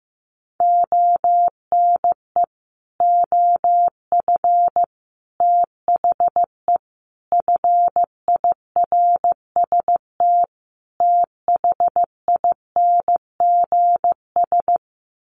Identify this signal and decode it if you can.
Morse code, heard as a keyed tone, about 15 wpm, 700 Hz. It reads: ONE OF THE FIRST THINGS
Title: Sets of 5 Words Encore 15wpm